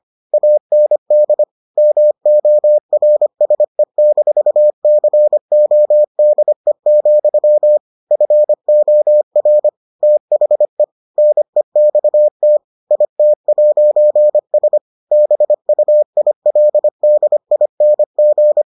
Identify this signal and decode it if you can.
Morse code, heard as a keyed tone, about 25 wpm, 600 Hz. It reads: AND MORSE-CODE, FOR THE NEXT IT'S BUILDING